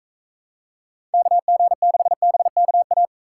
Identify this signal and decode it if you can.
Morse code, heard as a keyed tone, about 45 wpm, 700 Hz. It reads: KG6BKA